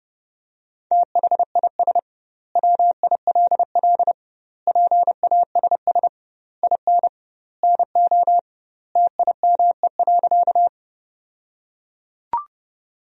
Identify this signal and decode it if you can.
Morse code, heard as a keyed tone, about 32 wpm, 700 Hz. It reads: THIS WILL PASS IN NO TIME.
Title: 2x Word-Spacing - Sentences from Top 400 Words - 30wpm